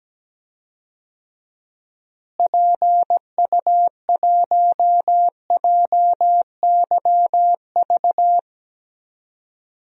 Morse code, heard as a keyed tone, about 17 wpm, 700 Hz. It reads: PU1JYV
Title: DXCC-100 Call Signs - 6 Characters - Repeated 2x - 17wpm